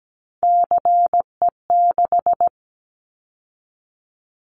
Morse code, heard as a keyed tone, about 17 wpm, 700 Hz. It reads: CE6
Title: Callsign Prefixes for Top 100 DXCC - 17wpm